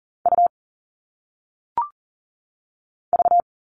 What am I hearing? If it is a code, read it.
Morse code, heard as a keyed tone, about 39 wpm, 700 Hz. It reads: U V
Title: All Cut Numbers - 1-Digit 40wpm